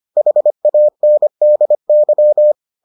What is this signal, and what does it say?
Morse code, heard as a keyed tone, about 25 wpm, 600 Hz. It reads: HANDY